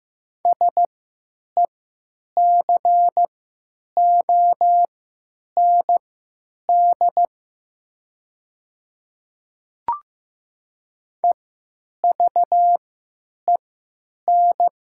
Morse code, heard as a keyed tone, about 15 wpm, spaced as for 9 wpm, 700 Hz. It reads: SECOND EVEN
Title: 3x Character Spacing - Top 300 Words - 15wpm